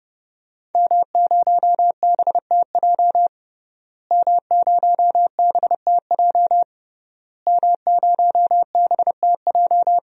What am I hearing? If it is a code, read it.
Morse code, heard as a keyed tone, about 30 wpm, 700 Hz. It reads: M0BTJ M0BTJ M0BTJ
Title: DXCC-100 Call Signs - 5 Characters - Repeated 3x - 30wpm